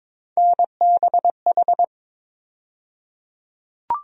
Morse code, heard as a keyed tone, about 22 wpm, 700 Hz. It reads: NBH
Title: USA Call Sign Postfixes - 3 Letters 22wpm